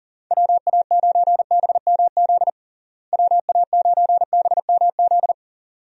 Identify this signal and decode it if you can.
Morse code, heard as a keyed tone, about 40 wpm, 700 Hz. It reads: WA9BMZ WA9BMZ
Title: USA Call Signs - 6 Characters - Repeated 2x - 40wpm